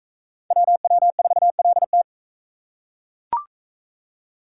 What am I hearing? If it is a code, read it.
Morse code, heard as a keyed tone, about 39 wpm, 700 Hz. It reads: WW4LT